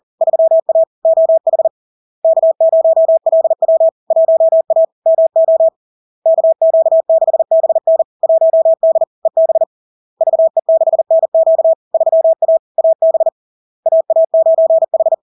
Morse code, heard as a keyed tone, about 40 wpm, 650 Hz. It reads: 3A OH K0LW 1A MO KQ6BN 1D EB VE6NQ 3A AB AA9H
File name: LARK-FD-2022-Log-in-CW-40wpm-650-Hz